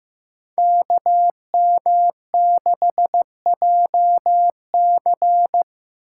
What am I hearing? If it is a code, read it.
Morse code, heard as a keyed tone, about 15 wpm, 700 Hz. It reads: KM6JC
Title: Code Groups - 5 Letter:Numbers - 15wpm